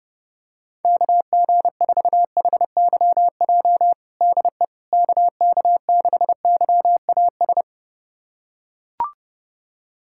Morse code, heard as a keyed tone, about 30 wpm, 700 Hz. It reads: KG4HYJ DE KK6YAS
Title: USA Callsign DE Callsign - 6 Characters - 30wpm